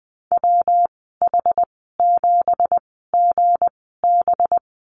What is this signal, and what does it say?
Morse code, heard as a keyed tone, about 20 wpm, spaced as for 15 wpm, 700 Hz. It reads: WH7GB
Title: 186 Lesson 30g - Letter b - Call Signs 20wpm